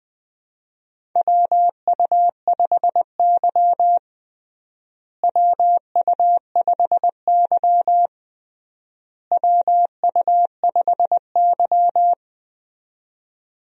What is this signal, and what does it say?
Morse code, heard as a keyed tone, about 20 wpm, 700 Hz. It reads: WU5Y WU5Y WU5Y
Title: USA Call Signs - 4 Characters - Repeated 3x - 20wpm